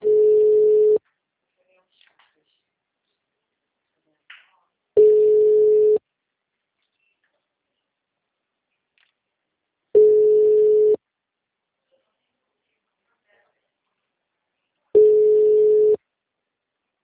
Ringback tone (Europe, 425 Hz).